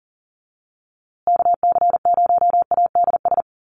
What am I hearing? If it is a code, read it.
Morse code, heard as a keyed tone, about 40 wpm, 700 Hz. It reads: KC0ADS
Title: USA Call Signs Repeated 2x 40wpm